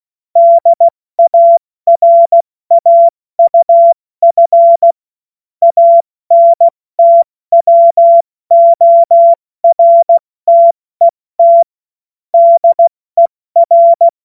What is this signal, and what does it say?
Morse code, heard as a keyed tone, about 16 wpm, 680 Hz. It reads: DARAUF ANTWORTET DER
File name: Morsekurs / Witze / Witz 21. Dezember 2021, 70 F80